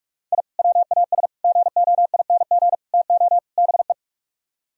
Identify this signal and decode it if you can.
Morse code, heard as a keyed tone, about 49 wpm, 700 Hz. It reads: I WAS GOING TO BE